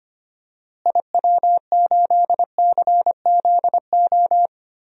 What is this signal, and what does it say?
Morse code, heard as a keyed tone, about 25 wpm, 700 Hz. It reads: IW8CZO
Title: DXCC-100 Call Signs - 6 Characters - Repeated 2x - 25wpm